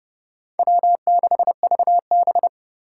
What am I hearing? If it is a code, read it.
Morse code, heard as a keyed tone, about 29 wpm, 700 Hz. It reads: W6VB